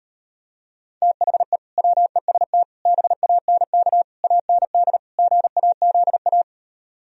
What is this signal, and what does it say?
Morse code, heard as a keyed tone, about 39 wpm, 700 Hz. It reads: THE WEST BANK AND GAZA